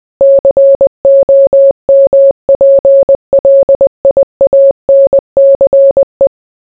Morse code, heard as a keyed tone, about 20 wpm, 550 Hz. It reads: COMPLIANCE